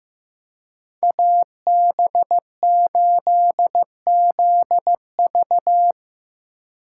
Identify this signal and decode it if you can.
Morse code, heard as a keyed tone, about 18 wpm, 700 Hz. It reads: AB8ZV
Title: Speed Racing - USA 2x2 Call Signs 15wpm